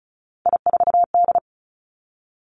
Morse code, heard as a keyed tone, about 36 wpm, 700 Hz. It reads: I4D
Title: Three Challenging Letter-Number Combinations 35wpm